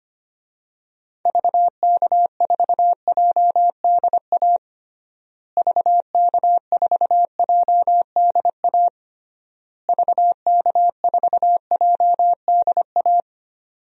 Morse code, heard as a keyed tone, about 25 wpm, 700 Hz. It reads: VK4JDA VK4JDA VK4JDA
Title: DXCC-100 Call Signs - 6 Characters - Repeated 3x - 25wpm